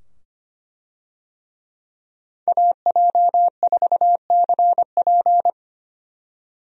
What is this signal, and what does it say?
Morse code, heard as a keyed tone, about 25 wpm, 700 Hz. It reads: AJ4CP
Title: USA Call Signs - 5 Characters - 25wpm